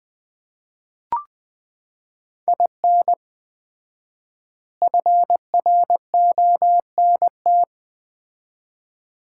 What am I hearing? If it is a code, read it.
Morse code, heard as a keyed tone, about 20 wpm, 700 Hz. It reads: IN FRONT
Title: 4x Word-Spacing - Sets of 4 Words - 20wpm